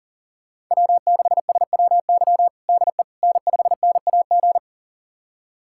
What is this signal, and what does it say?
Morse code, heard as a keyed tone, about 40 wpm, 700 Hz. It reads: W6SWY DE N5NAG
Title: USA Callsign DE Callsign 40wpm